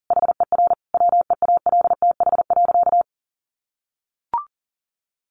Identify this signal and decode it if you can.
Morse code, heard as a keyed tone, about 40 wpm, 700 Hz. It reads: HER WEALTH.
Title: Sentences from Top 1500 Words 40wpm